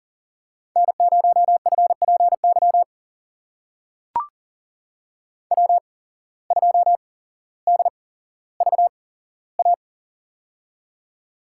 Morse code, heard as a keyed tone, about 40 wpm, spaced as for 17 wpm, 700 Hz. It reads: N0FPY W2DVA
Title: USA Call Signs - 5 Characters - 8x Character Spacing - 40wpm